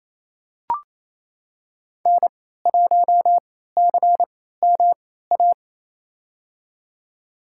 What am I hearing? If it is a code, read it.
Morse code, heard as a keyed tone, about 28 wpm, spaced as for 16 wpm, 700 Hz. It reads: N1CMA
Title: USA Call Signs - 3x Character Spacing - 28wpm